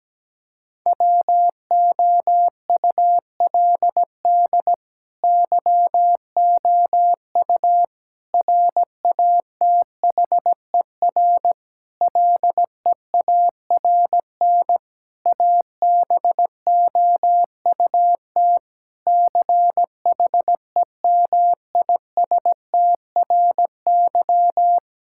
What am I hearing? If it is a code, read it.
Morse code, heard as a keyed tone, about 17 wpm, 700 Hz. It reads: WOULD YOU RATHER LEARN ABOUT CHEMISTRY
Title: Sentences from Top 1000 Words - Review Part 6 - 17wpm